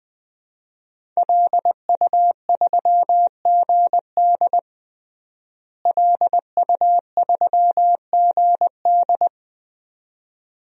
Morse code, heard as a keyed tone, about 20 wpm, 700 Hz. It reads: LU3GD LU3GD
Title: DXCC-100 Call Signs - 5 Characters - Repeated 2x - 20wpm